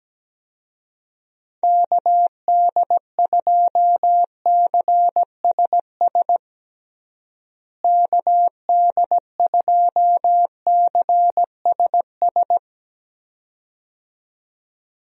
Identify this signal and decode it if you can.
Morse code, heard as a keyed tone, about 17 wpm, 700 Hz. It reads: KD2CSS KD2CSS
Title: USA Call Signs - 6 Characters - Repeated 2x - 17wpm